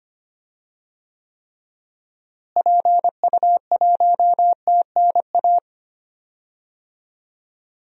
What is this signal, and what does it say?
Morse code, heard as a keyed tone, about 25 wpm, 700 Hz. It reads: PU1TNA
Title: DXCC-100 Call Signs - 6 Characters - Repeated 2x - 25wpm